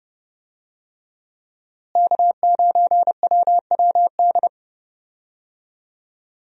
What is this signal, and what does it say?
Morse code, heard as a keyed tone, about 32 wpm, 700 Hz. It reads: K9WWD
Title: USA Call Signs - 5 Characters - Repeated 3x - 30wpm